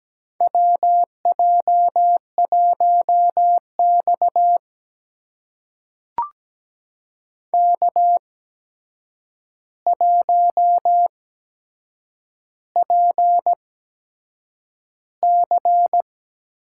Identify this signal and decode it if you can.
Morse code, heard as a keyed tone, about 17 wpm, spaced as for 7 wpm, 700 Hz. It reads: WJ1X K1PC
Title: USA Call Signs - 4 Characters - 8x Character Spacing - 17wpm